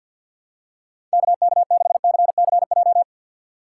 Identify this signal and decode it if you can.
Morse code, heard as a keyed tone, about 51 wpm, 690 Hz. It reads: KK6CCJ
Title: USA Call Signs - 2x Character Spacing - 50wpm